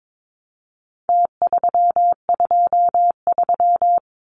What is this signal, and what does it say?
Morse code, heard as a keyed tone, about 22 wpm, 700 Hz. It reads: T323
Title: Numbers - Contest Style - 4-Digit 22wpm